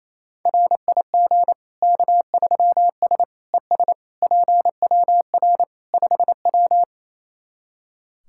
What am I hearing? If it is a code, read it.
Morse code, heard as a keyed tone, about 28 wpm, 700 Hz. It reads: RIG K3S ES PWR 5W
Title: K1USN SST QSO Practice 28wpm